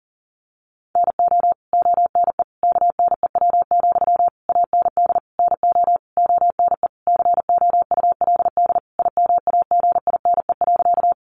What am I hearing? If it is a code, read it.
Morse code, heard as a keyed tone, about 40 wpm, 700 Hz. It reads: NO ONE KNEW, AND NO ONE COULD IMAGINE.
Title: Sentences from Top 1400 Words 40wpm